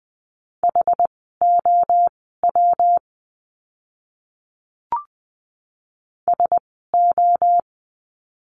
Morse code, heard as a keyed tone, about 20 wpm, spaced as for 14 wpm, 700 Hz. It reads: HOW SO
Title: 139 Lesson 23f - Letter f - Words 20wpm